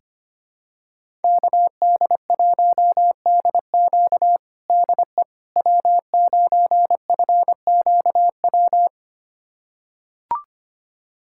Morse code, heard as a keyed tone, about 25 wpm, 700 Hz. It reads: KD1DQ DE W9FQW
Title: USA Callsign DE Callsign - 5 Characters - 25wpm